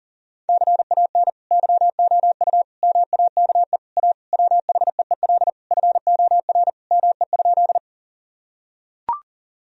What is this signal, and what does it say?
Morse code, heard as a keyed tone, about 40 wpm, 700 Hz. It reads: CAN YOU MAKE A WHEEL FOR ME?
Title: Sentences from Top 1000 Words - Review Part 3 - 40wpm